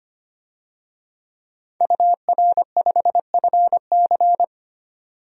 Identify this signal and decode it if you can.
Morse code, heard as a keyed tone, about 25 wpm, 700 Hz. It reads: UR5FC
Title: DXCC-100 Call Signs - 5 Characters - 25wpm